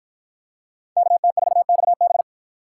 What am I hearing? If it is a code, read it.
Morse code, heard as a keyed tone, about 53 wpm, 700 Hz. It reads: KT4XB